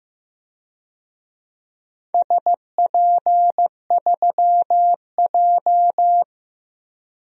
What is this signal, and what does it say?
Morse code, heard as a keyed tone, about 16 wpm, 700 Hz. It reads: SP3J
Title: DXCC-100 Call Signs - 4 Characters - Repeated 3x - 15wpm